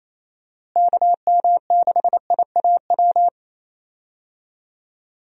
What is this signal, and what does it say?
Morse code, heard as a keyed tone, about 28 wpm, 700 Hz. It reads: KM6IAW
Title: USA Call Signs - 6 Characters - Repeated 2x - 28wpm